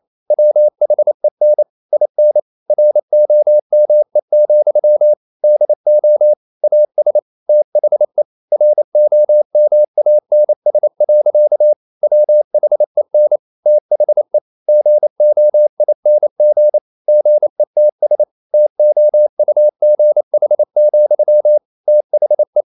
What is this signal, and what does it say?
Morse code, heard as a keyed tone, about 28 wpm, 600 Hz. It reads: WHEN IN ROME, DO AS THE ROMANS. WHEN THE GOING GETS TOUGH, THE
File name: CW / Practice / EnglishSayings28